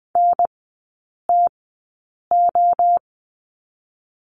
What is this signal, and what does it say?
Morse code, heard as a keyed tone, about 20 wpm, 700 Hz. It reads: N T O
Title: 029 Lesson 6e - Letter i - Three Characters 20wpm